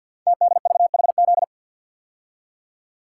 Morse code, heard as a keyed tone, about 52 wpm, 700 Hz. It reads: TDVHZ
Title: Code Groups - 5 Letters - 50wpm